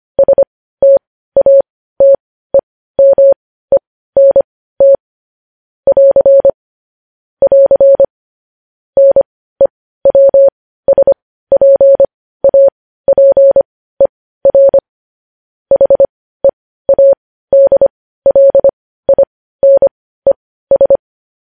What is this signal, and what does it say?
Morse code, heard as a keyed tone, about 25 wpm, spaced as for 15 wpm, 550 Hz. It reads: STATEMENT <AR> <AR> NEWSPAPER HEADLINES
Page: news headlines in morse code at 15 wpm